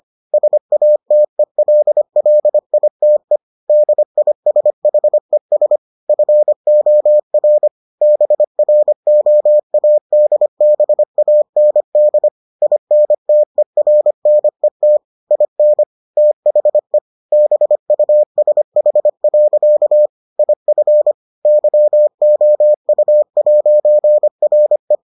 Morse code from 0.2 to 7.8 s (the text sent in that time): SATELLITE DISHES FOR